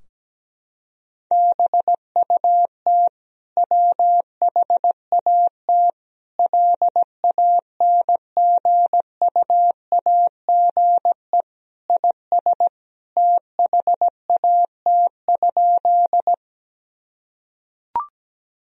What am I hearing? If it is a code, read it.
Morse code, heard as a keyed tone, about 17 wpm, 700 Hz. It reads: BUT WHAT LANGUAGE IS THAT?
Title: Sentences from Top 1000 Words - Review Part 3 - 17wpm